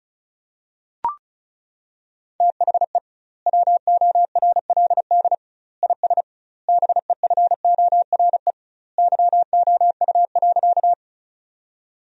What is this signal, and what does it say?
Morse code, heard as a keyed tone, about 36 wpm, 700 Hz. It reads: THE WORLD IS BEFORE YOU.
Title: 2x Word-Spacing - Sentences from Top 200 Words - 35wpm